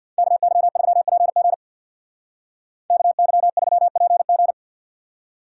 Morse code, heard as a keyed tone, about 55 wpm, 700 Hz. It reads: KY3PG KY3PG
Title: Speed Racing - USA 2x2 Call Signs 50wpm